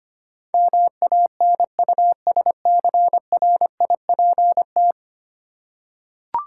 Morse code, heard as a keyed tone, about 25 wpm, 700 Hz. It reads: MANUSCRIPT
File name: Ten Letter Words - 25wpm